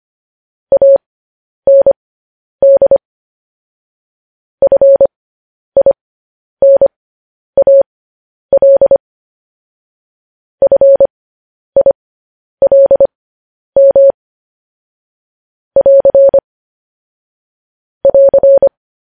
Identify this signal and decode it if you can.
Morse code, heard as a keyed tone, about 25 wpm, spaced as for 10 wpm, 550 Hz. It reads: AND FINAL FILM <AR> <AR>